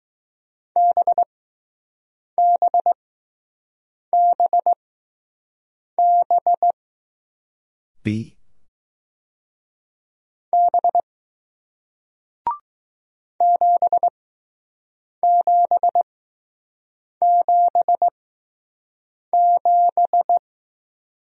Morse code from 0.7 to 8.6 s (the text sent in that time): BBBB